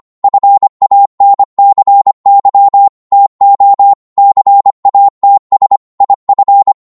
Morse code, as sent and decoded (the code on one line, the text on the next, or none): ..-. .- -. -.-. -.-- / - --- / -.-. .- - ... / .. ..-.
FANCY TO CATS IF